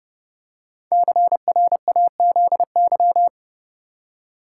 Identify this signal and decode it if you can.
Morse code, heard as a keyed tone, about 31 wpm, 700 Hz. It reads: CRAZY
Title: Five Letter Words - 30wpm